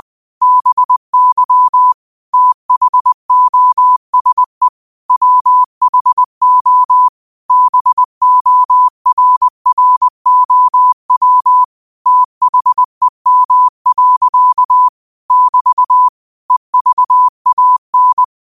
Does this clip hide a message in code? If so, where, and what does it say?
Yes: Morse code from 0.4 to 18.2 s, heard as a keyed tone, about 20 wpm, 1000 Hz. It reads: BY THOSE WHO BORROW THEM. <BT> EVAN